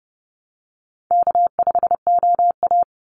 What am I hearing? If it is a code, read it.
Morse code, heard as a keyed tone, about 30 wpm, 700 Hz. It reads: K5OA
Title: Easy Call Signs Repeated 3x - 30wpm